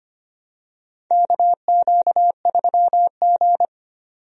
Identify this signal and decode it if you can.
Morse code, heard as a keyed tone, about 25 wpm, 700 Hz. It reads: KQ3G
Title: USA Call Signs - 4 Characters - 25wpm